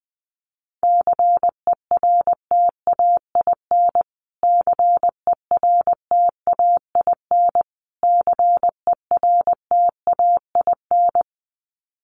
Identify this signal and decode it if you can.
Morse code, heard as a keyed tone, about 20 wpm, 700 Hz. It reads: CERTAIN CERTAIN CERTAIN
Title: Top 500 Words Repeated 20wpm